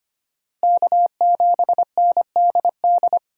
Morse code, heard as a keyed tone, about 25 wpm, 700 Hz. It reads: K7NDD
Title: USA Call Signs - 5 Characters - Repeated 3x - 25wpm